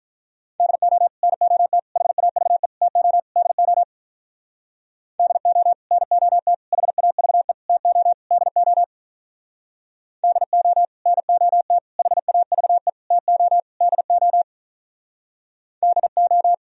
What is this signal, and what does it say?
Morse code, heard as a keyed tone, about 45 wpm, 700 Hz. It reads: DO NOT HAVE TO DO DO NOT HAVE TO DO DO NOT HAVE TO DO DO